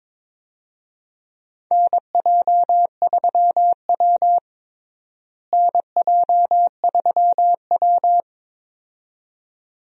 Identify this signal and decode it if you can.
Morse code, heard as a keyed tone, about 22 wpm, 700 Hz. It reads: NJ3W NJ3W
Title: USA Call Signs - 4 Characters - Repeated 2x - 22wpm